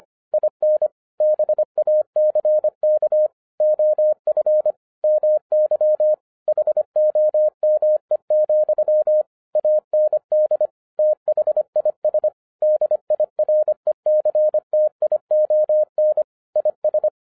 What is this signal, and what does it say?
Morse code, heard as a keyed tone, about 25 wpm, 610 Hz. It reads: IN BACK OF MY HOME, AND THIS DIRECTION IS